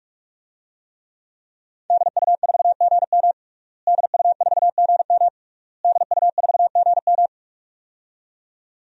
Morse code, heard as a keyed tone, about 46 wpm, 700 Hz. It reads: DU4GM DU4GM DU4GM
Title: DXCC-100 Call Signs - 5 Characters - Repeated 3x - 45wpm